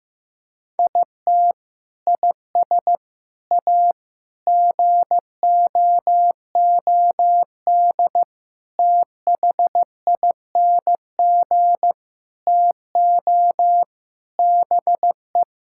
Morse code, heard as a keyed tone, about 15 wpm, 700 Hz. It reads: IT IS A GOOD THING TO BE
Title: Sentences from Top 1000 Words - Review Part 5 - 15wpm